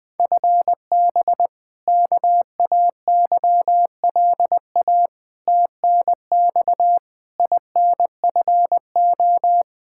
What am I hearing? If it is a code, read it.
Morse code, heard as a keyed tone, about 20 wpm, 700 Hz. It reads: FB KAYLA TNX INFO
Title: K1USN SST QSO Practice 20wpm